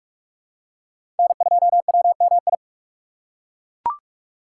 Morse code, heard as a keyed tone, about 41 wpm, 700 Hz. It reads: N2WMI
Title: USA Call Signs - 5 Characters - 2x Character Spacing - 45wpm